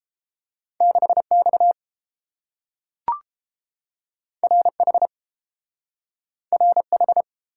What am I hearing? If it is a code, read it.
Morse code, heard as a keyed tone, about 30 wpm, 700 Hz. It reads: 6X RH RH